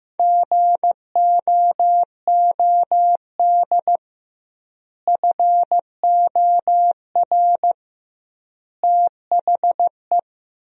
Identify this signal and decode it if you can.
Morse code, heard as a keyed tone, about 15 wpm, 700 Hz. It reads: GOOD FOR THE
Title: 2x Word-Spacing - Sentences from Top 300 Words - 15wpm